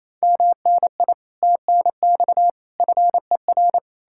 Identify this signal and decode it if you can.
Morse code, heard as a keyed tone, about 37 wpm, 700 Hz. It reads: MNI TNX FER